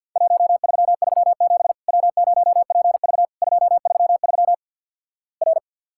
Morse code, heard as a keyed tone, about 50 wpm, 700 Hz. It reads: 1337 W0PV 233 R
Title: Worked All Europe DX Contest - QTC Exchange - 50wpm